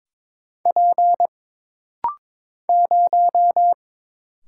Morse code, heard as a keyed tone, about 22 wpm, 700 Hz. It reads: P 0
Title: Single Letter-Number Alt - Rapid-Fire 22wpm